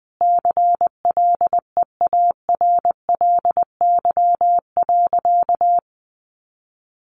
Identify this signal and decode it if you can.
Morse code, heard as a keyed tone, about 20 wpm, 700 Hz. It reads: CLEARLY.
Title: Sentences from Top 1300 Words 20wpm